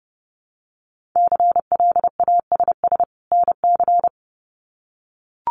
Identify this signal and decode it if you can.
Morse code, heard as a keyed tone, about 30 wpm, 700 Hz. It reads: CLASS NC
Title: Winter Field Day Exchanges 30wpm